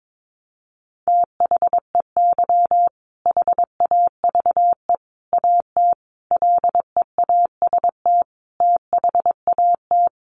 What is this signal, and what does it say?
Morse code, heard as a keyed tone, about 22 wpm, 700 Hz. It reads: THEY HAVE AT LEAST THAT
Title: Sentences from Top 900 Words 22wpm